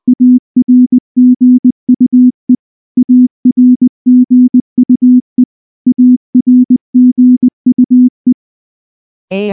Morse code, heard as a keyed tone, about 19 wpm, 250 Hz. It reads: ARGUE ARGUE ARGUE
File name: morse / morse-training-20WPM-2